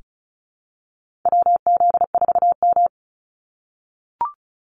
Morse code, heard as a keyed tone, about 36 wpm, 700 Hz. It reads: WZ4M
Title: NA SOTA Activator Callsigns 35wpm